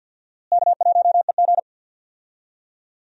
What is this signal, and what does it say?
Morse code, heard as a keyed tone, about 47 wpm, 700 Hz. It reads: K1EG
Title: USA Call Signs - 4 Characters - Repeated 3x - 50wpm